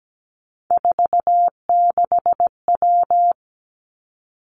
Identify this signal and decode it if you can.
Morse code, heard as a keyed tone, about 17 wpm, 700 Hz. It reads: 46W
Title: Three Challenging Letter-Number Combinations 17wpm